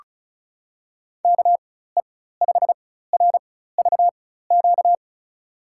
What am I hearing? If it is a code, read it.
Morse code, heard as a keyed tone, about 36 wpm, spaced as for 16 wpm, 700 Hz. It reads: KE5RVQ
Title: USA Call Signs - 6 Characters - 4x Character Spacing - 35wpm